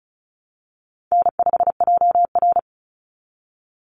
Morse code, heard as a keyed tone, about 34 wpm, 700 Hz. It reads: N5JR
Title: Call Signs Repeated 2x - 35wpm